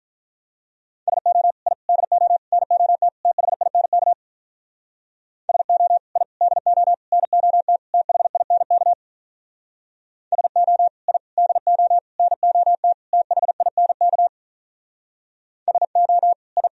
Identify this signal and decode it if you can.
Morse code, heard as a keyed tone, about 45 wpm, 700 Hz. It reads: SO I DO NOT THINK SO I DO NOT THINK SO I DO NOT THINK SO I